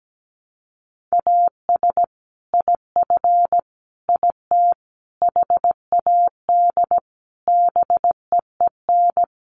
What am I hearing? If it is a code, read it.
Morse code, heard as a keyed tone, about 17 wpm, 700 Hz. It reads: AS IF IT HAD BEEN
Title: Sets of 5 Words Encore2 17wpm